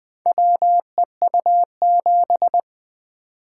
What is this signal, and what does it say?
Morse code, heard as a keyed tone, about 21 wpm, 700 Hz. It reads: WEU7
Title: Four Letter-Number Combinations - 20wpm